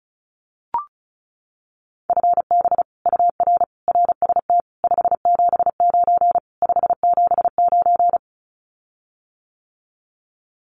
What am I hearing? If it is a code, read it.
Morse code, heard as a keyed tone, about 34 wpm, 700 Hz. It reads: FB UR RST 579 579
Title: Common QSO Phrases Encore 35wpm